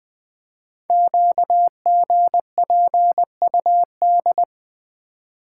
Morse code, heard as a keyed tone, about 20 wpm, 700 Hz. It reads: QGPUD
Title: Code Groups - 5 Letters - 20wpm